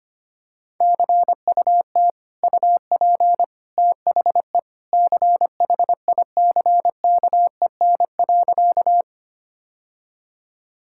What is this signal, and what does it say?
Morse code, heard as a keyed tone, about 25 wpm, 700 Hz. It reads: CUT UP THE CHICKEN.
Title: Sentences from Top 1000 Words - Review Part 5 - 25wpm